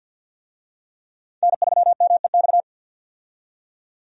Morse code, heard as a keyed tone, about 49 wpm, 700 Hz. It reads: N3MEX
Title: USA Call Signs - 5 Characters - 50wpm